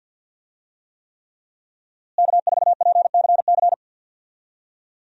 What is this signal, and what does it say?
Morse code, heard as a keyed tone, about 46 wpm, 700 Hz. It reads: K4PCC
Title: USA Call Signs - 5 Characters - Repeated 3x - 50wpm